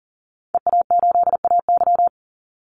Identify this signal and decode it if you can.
Morse code, heard as a keyed tone, about 40 wpm, 700 Hz. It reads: EA8AY
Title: Call Signs Repeated 3x - 40wpm